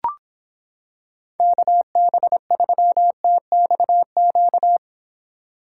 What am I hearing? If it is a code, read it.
Morse code, heard as a keyed tone, about 26 wpm, 700 Hz. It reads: KB3TXQ